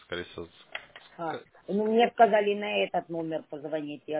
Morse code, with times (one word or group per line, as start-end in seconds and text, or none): none